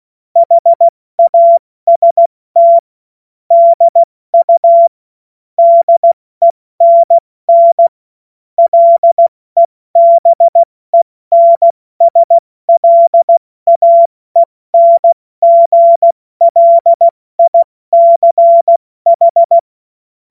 Morse code, heard as a keyed tone, about 16 wpm, 680 Hz. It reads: HAST DU DENN LEBENSLAENGLICH
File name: Morsekurs / Witze / Witz 01. März 2023 70 F80